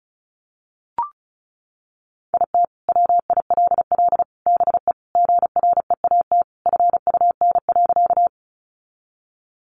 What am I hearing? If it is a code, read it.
Morse code, heard as a keyed tone, about 34 wpm, 700 Hz. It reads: IT WILL BE GREAT FUN.
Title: Sentences from Top 700 Words 35wpm